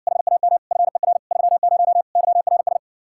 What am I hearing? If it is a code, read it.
Morse code, heard as a keyed tone, about 61 wpm, 700 Hz. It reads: HAM FER 30 YRS